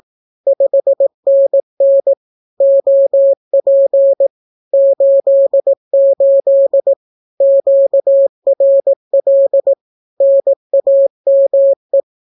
Morse code, heard as a keyed tone, about 18 wpm, 550 Hz. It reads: 5NN OP 88 QRL NAME
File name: naukapodstaw / skroty / skroty